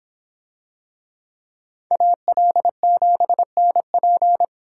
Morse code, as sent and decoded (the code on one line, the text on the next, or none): .- .-.. --... -. .--.
AL7NP